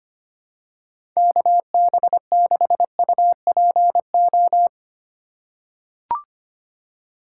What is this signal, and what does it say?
Morse code, heard as a keyed tone, about 25 wpm, 700 Hz. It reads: KB6UPO
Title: USA Call Signs - 6 Characters - 2x Character Spacing - 25wpm